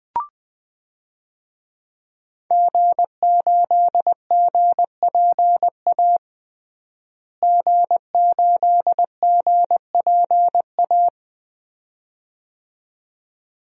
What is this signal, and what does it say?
Morse code, heard as a keyed tone, about 20 wpm, 700 Hz. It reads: G8GPA G8GPA
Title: DXCC-100 Call Signs - 5 Characters - Repeated 2x - 20wpm